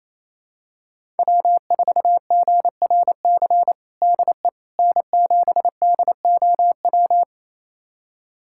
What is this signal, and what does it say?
Morse code, heard as a keyed tone, about 28 wpm, 700 Hz. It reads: W4GRC DE N7DOW
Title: USA Callsign DE Callsign - 5 Characters - 28wpm